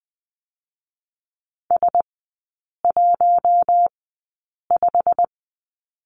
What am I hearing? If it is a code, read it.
Morse code, heard as a keyed tone, about 20 wpm, spaced as for 8 wpm, 700 Hz. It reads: S15
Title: 107 Lesson 18e - Letter m - Three Characters 20wpm